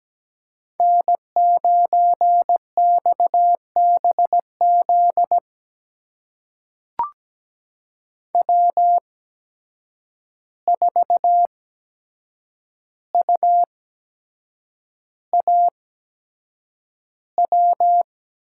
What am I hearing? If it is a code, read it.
Morse code, heard as a keyed tone, about 17 wpm, spaced as for 7 wpm, 700 Hz. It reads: N9XBZ W4UAW
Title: USA Call Signs - 5 Characters - 8x Character Spacing - 17wpm